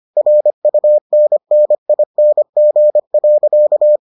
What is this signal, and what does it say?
Morse code, heard as a keyed tone, about 25 wpm, 600 Hz. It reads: RUNNING.